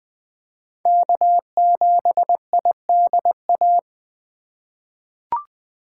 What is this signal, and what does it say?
Morse code, heard as a keyed tone, about 20 wpm, 700 Hz. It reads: K7IDA
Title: USA Call Signs - 5 Characters - 3x Character Spacing - 20wpm